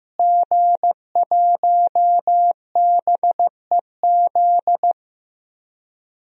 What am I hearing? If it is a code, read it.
Morse code, heard as a keyed tone, about 15 wpm, 700 Hz. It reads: G1BEZ
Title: DXCC-100 Call Signs - 5 Characters - Repeated 2x - 15wpm